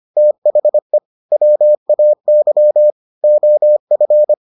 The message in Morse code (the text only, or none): THE WAY OF